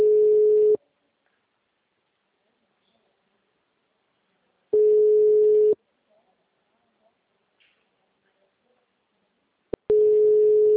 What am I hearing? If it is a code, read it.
Ringback tone (Europe, 425 Hz), with speech.